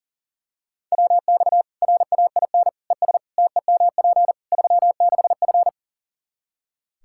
Morse code, heard as a keyed tone, about 40 wpm, 700 Hz. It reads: WX RAIN ES TEMP 36F
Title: Practice QSO - Exchange 2 - Info - 40wpm